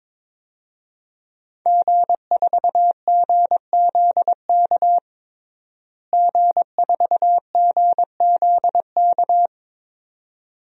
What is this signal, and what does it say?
Morse code, heard as a keyed tone, about 22 wpm, 700 Hz. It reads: G4GZK G4GZK
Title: DXCC-100 Call Signs - 5 Characters - Repeated 2x - 22wpm